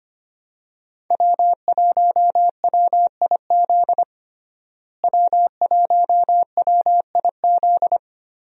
Morse code, heard as a keyed tone, about 25 wpm, 700 Hz. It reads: W1WIZ W1WIZ
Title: USA Call Signs - 5 Characters - Repeated 3x - 25wpm